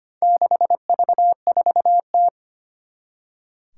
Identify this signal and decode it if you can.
Morse code, heard as a keyed tone, about 25 wpm, 700 Hz. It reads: 6V4T
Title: Four Challenging Letter-Number Combinations 25wpm